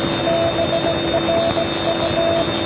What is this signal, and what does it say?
Morse code, heard as a keyed tone, about 17 wpm, 670 Hz, with noise. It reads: BRU